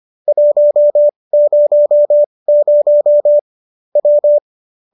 Morse code, heard as a keyed tone, about 25 wpm, spaced as for 20 wpm, 590 Hz. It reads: 100 W